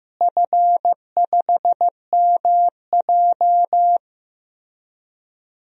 Morse code, heard as a keyed tone, about 15 wpm, 700 Hz. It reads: F5MJ
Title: DXCC-100 Call Signs - 4 Characters - Repeated 2x - 15wpm